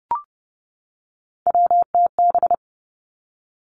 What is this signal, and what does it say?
Morse code, heard as a keyed tone, about 32 wpm, 700 Hz. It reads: WTB
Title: Three Challenging Letter-Number Combinations 30wpm